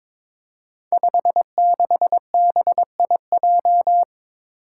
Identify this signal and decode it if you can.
Morse code, heard as a keyed tone, about 22 wpm, 700 Hz. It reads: 56BIJ
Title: Five Challenging Letter-Number Combinations 22wpm